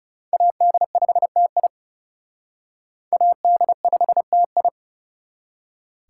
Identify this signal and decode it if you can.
Morse code, heard as a keyed tone, about 38 wpm, 700 Hz. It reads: AD5TI AD5TI